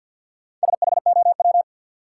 Morse code, heard as a keyed tone, about 50 wpm, 700 Hz. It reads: SHOW
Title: Sentences from Top 200 Words - New Word Drill - 50wpm